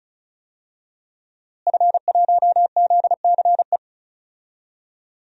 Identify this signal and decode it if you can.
Morse code, heard as a keyed tone, about 34 wpm, 700 Hz. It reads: F1ZCE
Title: DXCC-100 Call Signs - 5 Characters - Repeated 2x - 35wpm